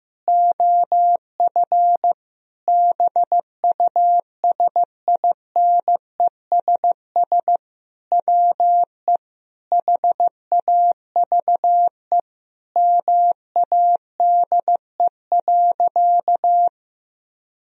Morse code, heard as a keyed tone, about 15 wpm, 700 Hz. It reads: OF BUSINESS WE HAVE MADE.
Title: Sentences from Top 1000 Words - Review Part 2 - 15wpm